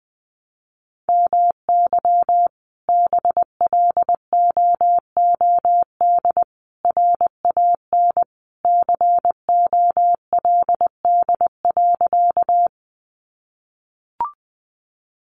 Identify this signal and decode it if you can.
Morse code, heard as a keyed tone, about 20 wpm, 700 Hz. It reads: MY BLOOD RAN COLD.
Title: Sentences from Top 1100 Words 20wpm